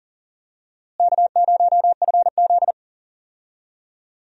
Morse code, heard as a keyed tone, about 37 wpm, 700 Hz. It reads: K0FZ
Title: USA Call Signs - 4 Characters - Repeated 2x - 40wpm